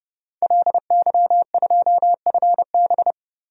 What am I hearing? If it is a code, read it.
Morse code, heard as a keyed tone, about 30 wpm, 700 Hz. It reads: LY2FB
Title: DXCC-100 Call Signs - 5 Characters - Repeated 2x - 30wpm